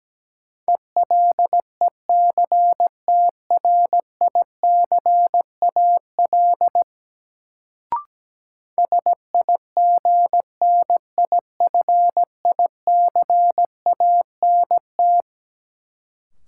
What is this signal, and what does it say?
Morse code, heard as a keyed tone, about 17 wpm, 700 Hz. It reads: ELECTRICAL SIGNIFICANT
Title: Sentences from Top 1000 Words - Review Part 6 - New Word Drill - 17wpm